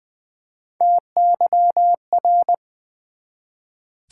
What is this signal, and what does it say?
Morse code, heard as a keyed tone, about 20 wpm, 700 Hz. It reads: TYR
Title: Build A Word - 1 Letter At A Time - 3 to 5-Letter Words - 20wpm